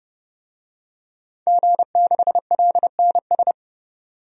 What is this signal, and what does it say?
Morse code, heard as a keyed tone, about 32 wpm, 700 Hz. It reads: G6LNS
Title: DXCC-100 Call Signs - 5 Characters - 30wpm